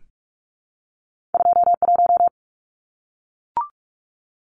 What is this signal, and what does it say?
Morse code, heard as a keyed tone, about 46 wpm, 700 Hz. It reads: 21
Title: Numbers - 2-Digit 45wpm